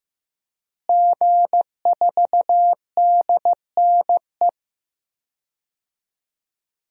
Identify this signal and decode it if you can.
Morse code, heard as a keyed tone, about 15 wpm, 700 Hz. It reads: G4DNE
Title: DXCC-100 Call Signs - 5 Characters - Repeated 3x - 15wpm